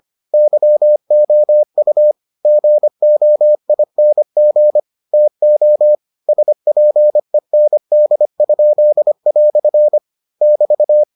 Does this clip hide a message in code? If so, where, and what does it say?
Yes: Morse code from 0.3 to 11.0 s, heard as a keyed tone, about 25 wpm, 600 Hz. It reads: YOU GOING TO SPEND?" <BT>